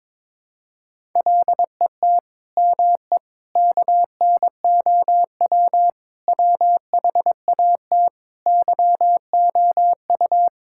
Morse code, heard as a keyed tone, about 22 wpm, 700 Hz. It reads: LET ME KNOW WHAT YOU
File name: Sets of 9 Words - 22wpm